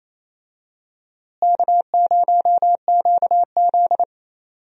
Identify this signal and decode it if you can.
Morse code, heard as a keyed tone, about 28 wpm, 700 Hz. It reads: K0QZ
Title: USA Call Signs - 4 Characters - Repeated 3x - 28wpm